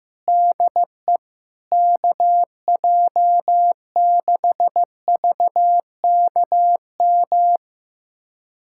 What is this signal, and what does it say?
Morse code, heard as a keyed tone, about 15 wpm, 700 Hz. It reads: DE KJ6VKM
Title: USA Callsign DE Callsign - 6 Characters - 15wpm